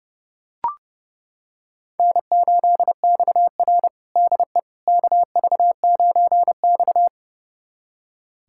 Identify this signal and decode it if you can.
Morse code, heard as a keyed tone, about 32 wpm, 700 Hz. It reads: N8XR DE KV9X
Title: USA Callsign DE Callsign - 4 Characters - 30wpm